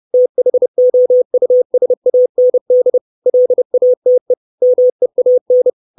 Morse code, heard as a keyed tone, about 30 wpm, 500 Hz. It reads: THOUSAND LATE MEAN
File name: 632 words / 632 Words 30WPM